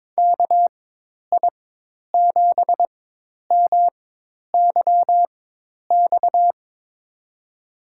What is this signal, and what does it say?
Morse code, heard as a keyed tone, about 22 wpm, spaced as for 10 wpm, 700 Hz. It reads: KI7MYX
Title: USA Call Signs - 6 Characters - 4x Character Spacing - 22wpm